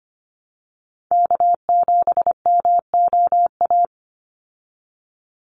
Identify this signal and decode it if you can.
Morse code, heard as a keyed tone, about 25 wpm, 700 Hz. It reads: K7MOA
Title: Call Signs Repeated 3x - 25wpm